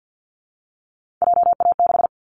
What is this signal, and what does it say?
Morse code, heard as a keyed tone, about 50 wpm, 700 Hz. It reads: JA6
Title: Callsign Prefixes for Top 25 DXCC - Repeated 3x - 50wpm